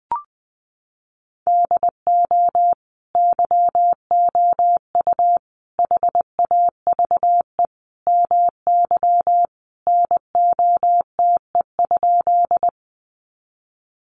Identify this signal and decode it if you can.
Morse code, heard as a keyed tone, about 20 wpm, 700 Hz. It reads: DO YOU HAVE MY NOTE?
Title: Sentences from Top 500 Words 20wpm